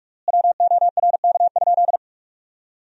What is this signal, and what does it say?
Morse code, heard as a keyed tone, about 42 wpm, 700 Hz. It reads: WORK?
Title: 3x Word-Spacing - Sentences from Top 500 Words - 45wpm